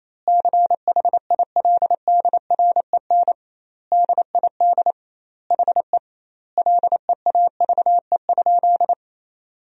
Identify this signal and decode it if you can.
Morse code, heard as a keyed tone, about 28 wpm, 700 Hz. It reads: CHILDREN DID HE LEAVE?
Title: 2x Word-Spacing - Sentences from Top 400 Words - 28wpm